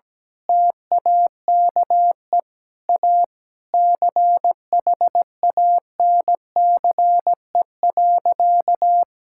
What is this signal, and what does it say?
Morse code, heard as a keyed tone, about 17 wpm, 700 Hz. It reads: TAKE A CHANCE.
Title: Sentences from Top 1000 Words - Review Part 6 - 17wpm